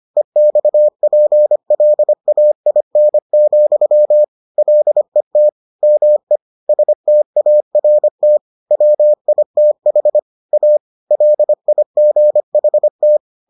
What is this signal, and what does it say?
Morse code, heard as a keyed tone, about 25 wpm, 600 Hz. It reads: EXPLAIN, LET ME START WITH A LIGHT